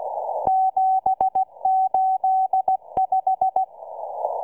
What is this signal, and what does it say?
Morse code, heard as a keyed tone, about 16 wpm, 760 Hz. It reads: 785